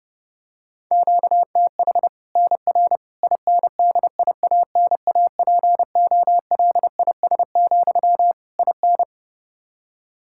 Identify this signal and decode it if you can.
Morse code, heard as a keyed tone, about 30 wpm, 700 Hz. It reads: QTH NR INDIANAPOLIS, IN
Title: Practice QSO - Exchange 1 - Beginning - 30wpm